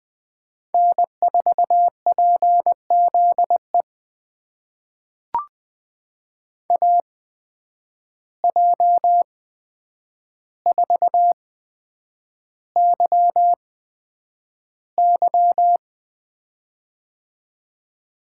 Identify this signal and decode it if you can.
Morse code, heard as a keyed tone, about 20 wpm, spaced as for 9 wpm, 700 Hz. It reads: N4PZE AJ4YY
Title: USA Call Signs - 8x Character Spacing - 20wpm